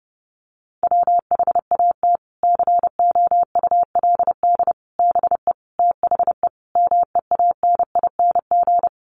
Morse code, heard as a keyed tone, about 30 wpm, 700 Hz. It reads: WHAT COULD BE THE MEANING
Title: Sentences from Top 900 Words 30wpm